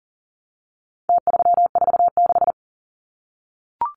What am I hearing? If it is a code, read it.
Morse code, heard as a keyed tone, about 40 wpm, 700 Hz. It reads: T346
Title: Numbers - Contest Style - 4-Digit 40wpm